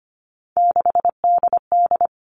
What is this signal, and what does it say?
Morse code, heard as a keyed tone, about 25 wpm, 700 Hz. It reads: 6DD
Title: All Cut Numbers - 3-Digit 25wpm